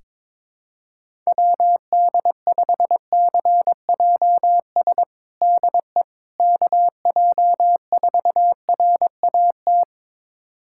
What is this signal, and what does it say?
Morse code, heard as a keyed tone, about 22 wpm, 700 Hz. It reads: WD5CJS DE KJ4RAT
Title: USA Callsign DE Callsign 22wpm